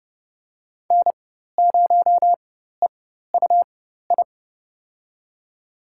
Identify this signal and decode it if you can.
Morse code, heard as a keyed tone, about 30 wpm, spaced as for 14 wpm, 700 Hz. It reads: N0EUI
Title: USA Call Signs - 5 Characters - 4x Character Spacing - 30wpm